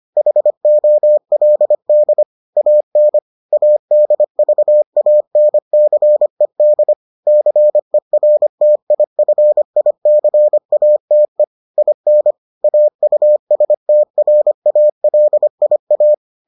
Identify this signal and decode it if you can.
Morse code, heard as a keyed tone, about 25 wpm, 600 Hz. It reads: HOLD AN ADVANCED CERTIFICATE IN AUSTRALIA